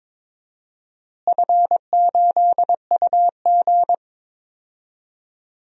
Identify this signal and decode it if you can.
Morse code, heard as a keyed tone, about 22 wpm, 700 Hz. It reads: F8UG
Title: DXCC-100 Call Signs - 4 Characters - Repeated 2x - 22wpm